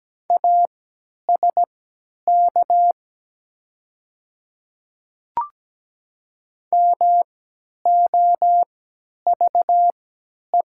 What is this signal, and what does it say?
Morse code, heard as a keyed tone, about 17 wpm, spaced as for 10 wpm, 700 Hz. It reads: ASK MOVE
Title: 3x Character Spacing - Top 300 Words - 17wpm